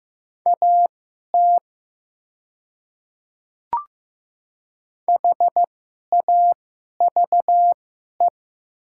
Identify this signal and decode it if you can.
Morse code, heard as a keyed tone, about 15 wpm, spaced as for 11 wpm, 700 Hz. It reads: AT HAVE
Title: 2x Character Spacing - Top 100 Words - 15wpm